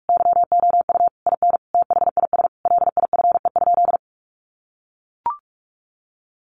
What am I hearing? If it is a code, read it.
Morse code, heard as a keyed tone, about 44 wpm, 700 Hz. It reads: YOU IN THIS LIFE?
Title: Sentences from Top 1200 Words 45wpm